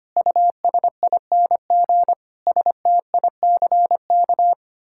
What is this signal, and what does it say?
Morse code, heard as a keyed tone, about 25 wpm, 700 Hz. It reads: USING STICK
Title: Pangram Challenge v2 - 25wpm